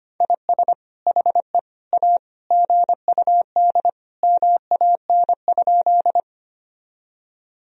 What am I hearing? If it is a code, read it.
Morse code, heard as a keyed tone, about 25 wpm, 700 Hz. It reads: IS HE A GUD MAN?
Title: Sentences from Top 200 Words and Abbreviations 25wpm